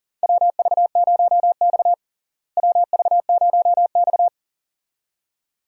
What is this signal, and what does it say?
Morse code, heard as a keyed tone, about 40 wpm, 700 Hz. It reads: WV0X WV0X
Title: USA Call Signs - 4 Characters - Repeated 2x - 40wpm